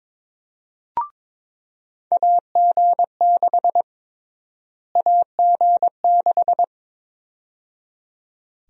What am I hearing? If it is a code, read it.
Morse code, heard as a keyed tone, about 22 wpm, 700 Hz. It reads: AG6 AG6
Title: USA Call Sign Prefixes - Repeated 2x - 22wpm